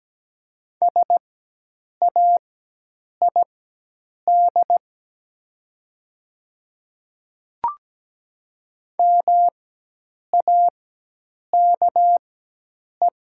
Morse code, heard as a keyed tone, about 17 wpm, spaced as for 8 wpm, 700 Hz. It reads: SAID MAKE
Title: 4x Character Spacing - Top 100 Words - 17wpm